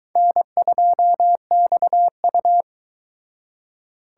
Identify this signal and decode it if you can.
Morse code, heard as a keyed tone, about 22 wpm, 700 Hz. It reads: N2XU